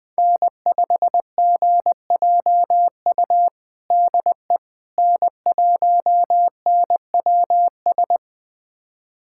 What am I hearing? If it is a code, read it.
Morse code, heard as a keyed tone, about 20 wpm, 700 Hz. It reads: N5GJU DE N1NWS
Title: USA Callsign DE Callsign - 5 Characters - 20wpm